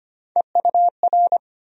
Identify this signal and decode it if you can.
Morse code, heard as a keyed tone, about 25 wpm, 700 Hz. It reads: EUR
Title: USA Call Sign Postfixes - 3 Letters 25wpm